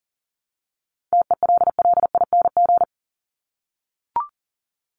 Morse code, heard as a keyed tone, about 40 wpm, 700 Hz. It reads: TELLING
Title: ING Suffix - Top 100 Words with Base Words 40wpm